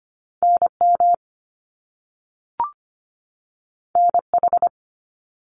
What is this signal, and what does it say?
Morse code, heard as a keyed tone, about 25 wpm, 700 Hz. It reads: NM NH
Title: US States 25wpm